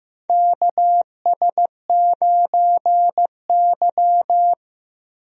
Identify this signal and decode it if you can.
Morse code, heard as a keyed tone, about 16 wpm, 700 Hz. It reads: KS9Y
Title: USA Call Signs - 4 Characters - Repeated 3x - 15wpm